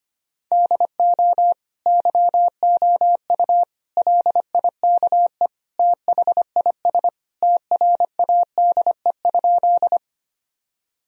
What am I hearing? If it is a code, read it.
Morse code, heard as a keyed tone, about 25 wpm, 690 Hz. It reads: DO YOU LIKE THIS TRADE?
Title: Sentences from Top 1000 Words - Review Part 4 - 25wpm